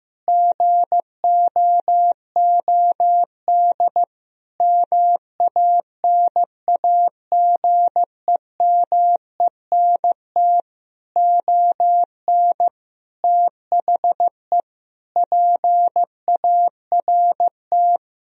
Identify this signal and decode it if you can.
Morse code, heard as a keyed tone, about 15 wpm, 700 Hz. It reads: GOOD MANAGEMENT ON THE PART
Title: Sentences from Top 1000 Words - Review Part 4 - 15wpm